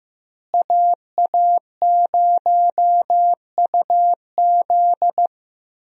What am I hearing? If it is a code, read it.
Morse code, heard as a keyed tone, about 14 wpm, 700 Hz. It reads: AA0UZ
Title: USA Call Signs - 5 Characters - Repeated 2x - 15wpm